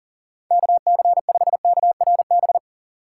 Morse code, heard as a keyed tone, about 41 wpm, 700 Hz. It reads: KC5KRB
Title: USA Call Signs - 6 Characters - 4x Character Spacing - 40wpm